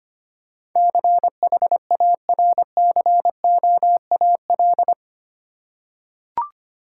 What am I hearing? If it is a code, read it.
Morse code, heard as a keyed tone, about 25 wpm, 690 Hz. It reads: CHARCOAL
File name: practice / 25wpm / Easy Words with Hints - 25wpm